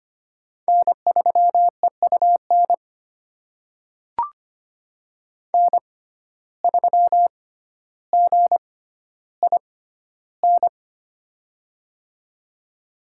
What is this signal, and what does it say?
Morse code, heard as a keyed tone, about 25 wpm, spaced as for 13 wpm, 700 Hz. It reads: N3EUN N3GIN
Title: USA Call Signs - 5 Characters - 6x Character Spacing - 25wpm